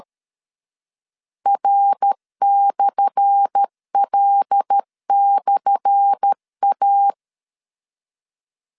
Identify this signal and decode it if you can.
Morse code, heard as a keyed tone, about 13 wpm, 800 Hz. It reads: R/L/A
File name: LEZIONI / 10 SIGNS / 10A SLASH 12wpm 2314